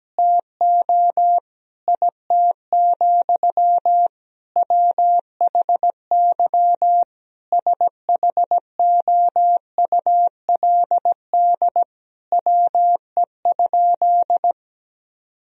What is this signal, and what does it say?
Morse code, heard as a keyed tone, about 17 wpm, 700 Hz. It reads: TO IT, WHY SHOULD WE?
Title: Sentences from Top 1000 Words - Review Part 3 - 17wpm